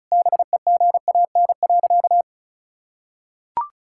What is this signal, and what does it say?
Morse code, heard as a keyed tone, about 34 wpm, 700 Hz. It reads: BEGAN.
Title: Sentences from Top 1000 Words - Review Part 5 - 35wpm